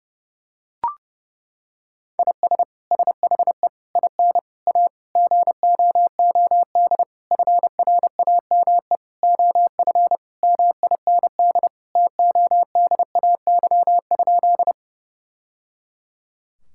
Morse code, heard as a keyed tone, about 30 wpm, 700 Hz. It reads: IS SHE IN A GOOD FRAME OF MIND TODAY?
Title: Sentences from Top 1000 Words - Review Part 6 - 30wpm